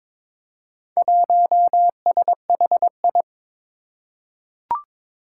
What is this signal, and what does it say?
Morse code, heard as a keyed tone, about 22 wpm, 700 Hz. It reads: 1SHI
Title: Four Challenging Letter-Number Combinations 22wpm